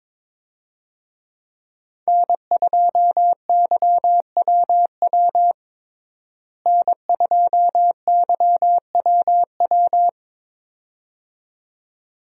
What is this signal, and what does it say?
Morse code, heard as a keyed tone, about 22 wpm, 700 Hz. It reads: N2YWW N2YWW
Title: USA Call Signs - 5 Characters - Repeated 2x - 22wpm